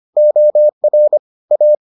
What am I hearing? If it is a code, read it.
Morse code, heard as a keyed tone, about 25 wpm, 600 Hz. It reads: OR A